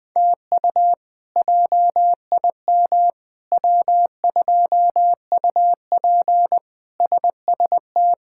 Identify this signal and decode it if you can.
Morse code, heard as a keyed tone, about 21 wpm, 700 Hz. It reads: TU JIM W2UP SST
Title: K1USN SST QSO Practice - 20wpm